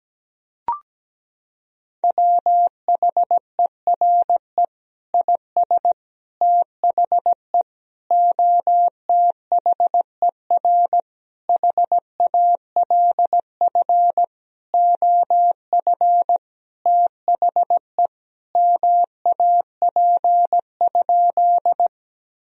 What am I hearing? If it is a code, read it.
Morse code, heard as a keyed tone, about 17 wpm, 700 Hz. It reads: WHERE IS THE OTHER HALF OF THE MAP?
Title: Sentences from Top 1000 Words - Review Part 2 - 17wpm